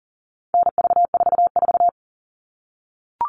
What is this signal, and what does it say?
Morse code, heard as a keyed tone, about 40 wpm, 700 Hz. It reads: NV44
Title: All Cut Numbers - 4-Digit 40wpm